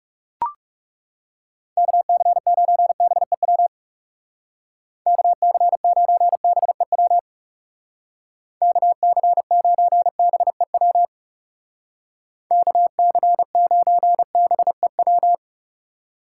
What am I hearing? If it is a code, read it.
Morse code, heard as a keyed tone, about 36 wpm, 700 Hz. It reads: KC9BEW KC9BEW KC9BEW KC9BEW
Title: Speed Racing - USA 2x3 Call Signs 30wpm